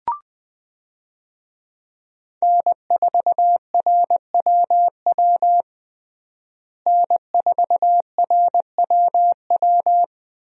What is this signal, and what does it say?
Morse code, heard as a keyed tone, about 20 wpm, 700 Hz. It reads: N4RWW N4RWW
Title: USA Call Signs - 5 Characters - Repeated 2x - 20wpm